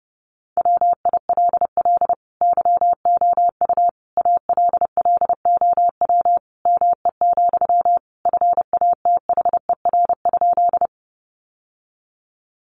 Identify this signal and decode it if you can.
Morse code, heard as a keyed tone, about 30 wpm, 700 Hz. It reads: WILL YOU ALLOW ME, FATHER?
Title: Sentences from Top 700 Words 30wpm